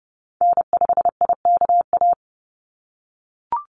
Morse code, heard as a keyed tone, about 30 wpm, 700 Hz. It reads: N5IKA
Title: USA Call Signs 30wpm